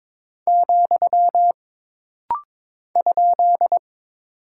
Morse code, heard as a keyed tone, about 22 wpm, 700 Hz. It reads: , ?
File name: practice / 22wpm / Single Punctuation-Prosign - Rapid-Fire - 22wpm